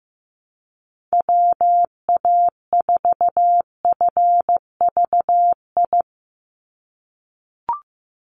Morse code, heard as a keyed tone, about 15 wpm, 700 Hz. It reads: WA4FVI
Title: USA Call Signs 15wpm